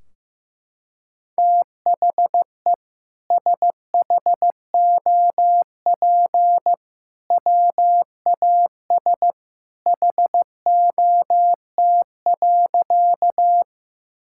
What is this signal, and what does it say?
Morse code, heard as a keyed tone, about 15 wpm, 700 Hz. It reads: THE SHOP WAS HOT.
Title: Sentences from Top 1600 Words 15wpm